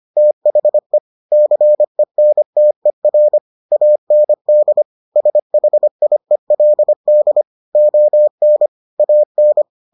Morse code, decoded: THE CENTER AND SHIELD ON AN

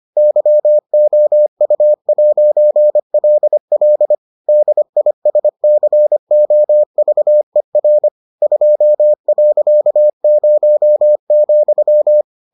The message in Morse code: YOU'LL DISCOVER 2.0,

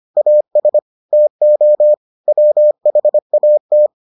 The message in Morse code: AS TO WHAT

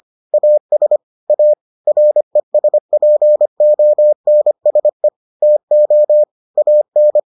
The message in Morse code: AS A RESPONSE TO AN

